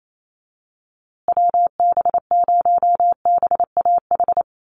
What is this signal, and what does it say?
Morse code, heard as a keyed tone, about 28 wpm, 700 Hz. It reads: WB0BAH
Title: USA Call Signs Repeated 3x 28wpm